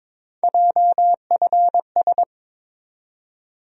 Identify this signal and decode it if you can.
Morse code, heard as a keyed tone, about 22 wpm, 700 Hz. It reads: JFS
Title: USA Call Sign Postfixes - 3 Letters 22wpm